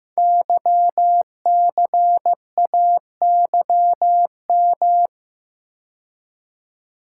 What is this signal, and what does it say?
Morse code, heard as a keyed tone, about 15 wpm, 700 Hz. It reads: YCAYM
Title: Code Groups - 5 Letters - 15wpm